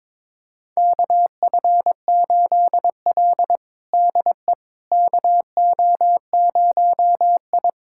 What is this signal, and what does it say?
Morse code, heard as a keyed tone, about 22 wpm, 700 Hz. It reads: KF8L DE KO0I
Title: USA Callsign DE Callsign - 4 Characters - 22wpm